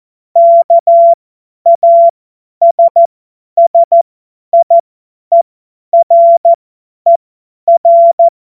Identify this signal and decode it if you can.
Morse code, heard as a keyed tone, about 14 wpm, spaced as for 10 wpm, 680 Hz. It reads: KASSIERER
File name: Morsekurs / Witze / Witz 23. Mai 2023 50 F70